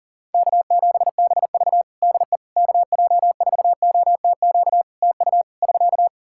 Morse code, heard as a keyed tone, about 40 wpm, 700 Hz. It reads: K7BV DE KJ4OTQ TU <SK>
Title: Practice QSO - Exchange 4 - Ending - 40wpm